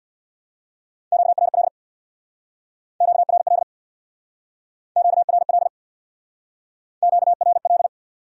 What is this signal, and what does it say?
Morse code, heard as a keyed tone, about 60 wpm, 700 Hz. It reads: QRL QRL QRL QRL
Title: Speed-Racing Q-Codes - 50wpm